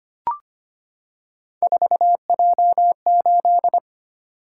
Morse code, heard as a keyed tone, about 25 wpm, 700 Hz. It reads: 4J8
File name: Three Letter-Number Combinations - 25wpm